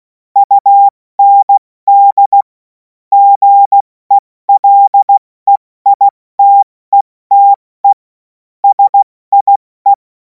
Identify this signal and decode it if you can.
Morse code, heard as a keyed tone, about 16 wpm, 800 Hz. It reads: UND GELEITETE SIE
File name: Morsekurs / Witze / Witz 31. März 2026 70 F80